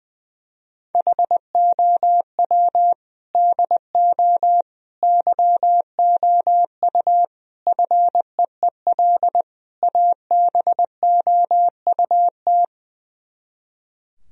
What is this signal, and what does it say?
Morse code, heard as a keyed tone, about 20 wpm, 700 Hz. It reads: HOW DO YOU FEEL ABOUT
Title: Speed-Racing - Sets of 5 Words 20wpm